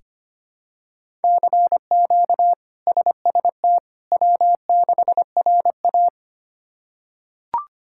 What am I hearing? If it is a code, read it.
Morse code, heard as a keyed tone, about 25 wpm, 700 Hz. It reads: CQ SST W6RA
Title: K1USN SST QSO Practice - 25wpm